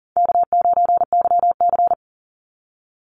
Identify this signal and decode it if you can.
Morse code, heard as a keyed tone, about 40 wpm, 700 Hz. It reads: K9YC
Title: Easy Call Signs Repeated 2x - 40wpm